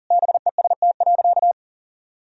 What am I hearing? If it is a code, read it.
Morse code, heard as a keyed tone, about 40 wpm, 700 Hz. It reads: BEST.
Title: Sentences from Top 1000 Words - Review Part 3 - 40wpm